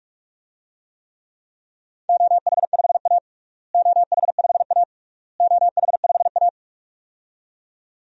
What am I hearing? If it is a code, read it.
Morse code, heard as a keyed tone, about 46 wpm, 700 Hz. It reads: OH5A OH5A OH5A
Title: DXCC-100 Call Signs - 4 Characters - Repeated 3x - 45wpm